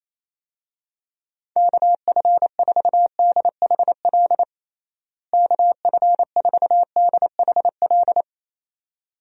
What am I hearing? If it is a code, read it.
Morse code, heard as a keyed tone, about 28 wpm, 700 Hz. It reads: KF4DHL KF4DHL
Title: USA Call Signs - 6 Characters - Repeated 2x - 28wpm